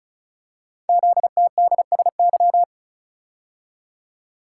Morse code, heard as a keyed tone, about 36 wpm, 690 Hz. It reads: ZTDSY
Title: Code Groups - 5 Letters - 35wpm